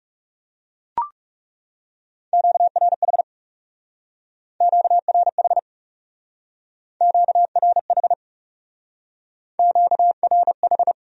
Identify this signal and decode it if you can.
Morse code, heard as a keyed tone, about 35 wpm, 700 Hz. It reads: QRH QRH QRH QRH
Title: Speed-Racing Q-Codes - 30wpm